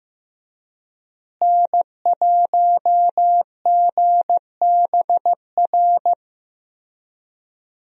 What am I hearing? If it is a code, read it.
Morse code, heard as a keyed tone, about 15 wpm, 700 Hz. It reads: N1GBR
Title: USA Call Signs - 5 Characters - Repeated 3x - 15wpm